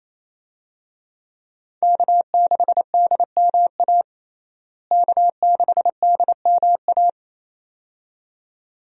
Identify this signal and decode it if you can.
Morse code, heard as a keyed tone, about 28 wpm, 700 Hz. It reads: K6DMA K6DMA
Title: USA Call Signs - 5 Characters - Repeated 2x - 28wpm